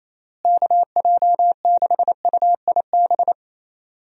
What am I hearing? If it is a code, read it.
Morse code, heard as a keyed tone, about 28 wpm, 700 Hz. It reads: KJ6UIB
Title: USA Call Signs - 6 Characters - Repeated 3x - 28wpm